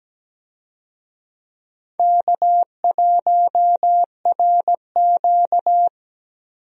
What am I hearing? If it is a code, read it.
Morse code, heard as a keyed tone, about 17 wpm, 700 Hz. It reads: K1RQ
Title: USA Call Signs - 4 Characters - Repeated 2x - 17wpm